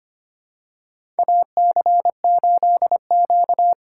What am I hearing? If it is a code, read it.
Morse code, heard as a keyed tone, about 25 wpm, 700 Hz. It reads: AC8Q
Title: USA Call Signs - 4 Characters - 4x Character Spacing - 25wpm